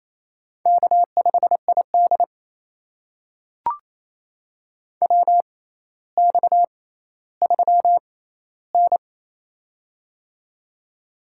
Morse code, heard as a keyed tone, about 28 wpm, spaced as for 14 wpm, 700 Hz. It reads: K5ID WX3N
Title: USA Call Signs - 4 Characters - 6x Character Spacing - 28wpm